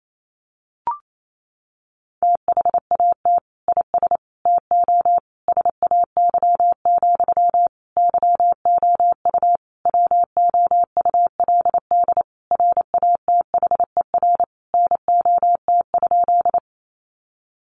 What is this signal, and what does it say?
Morse code, heard as a keyed tone, about 28 wpm, 700 Hz. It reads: THAT IS TO SAY, YOU WOULD RATHER NOT?
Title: Sentences from Top 700 Words 28wpm